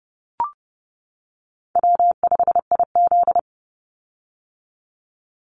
Morse code, heard as a keyed tone, about 30 wpm, 700 Hz. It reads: W5IZ
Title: Call Signs - 30wpm